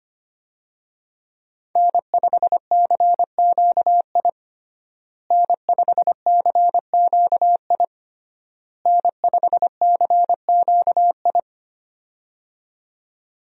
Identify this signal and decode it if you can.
Morse code, heard as a keyed tone, about 25 wpm, 700 Hz. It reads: N5CQI N5CQI N5CQI
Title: USA Call Signs - 5 Characters - Repeated 3x - 25wpm